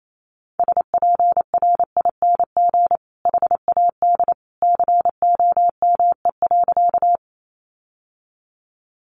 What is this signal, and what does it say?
Morse code, heard as a keyed tone, about 28 wpm, 700 Hz. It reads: SPRING HAD COME.
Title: Sentences from Top 1200 Words 28wpm